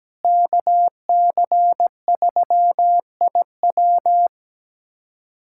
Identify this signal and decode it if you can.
Morse code, heard as a keyed tone, about 17 wpm, 700 Hz. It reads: KC3IW
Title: USA Callsign DE Callsign 17wpm